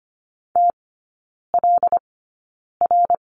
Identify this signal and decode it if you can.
Morse code, heard as a keyed tone, about 25 wpm, spaced as for 9 wpm, 700 Hz. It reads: TLR
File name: Lesson 15e - Number 5 - Three Characters 25wpm